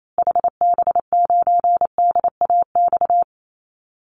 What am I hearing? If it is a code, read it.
Morse code, heard as a keyed tone, about 28 wpm, 700 Hz. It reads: HB9DAX
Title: Call Signs Repeated 2x - 28wpm